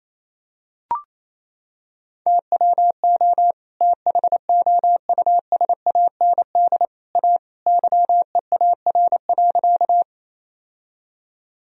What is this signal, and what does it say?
Morse code, heard as a keyed tone, about 28 wpm, 700 Hz. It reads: TWO THOUSAND A YEAR.
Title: Sentences from Top 1000 Words - Review Part 3 - 28wpm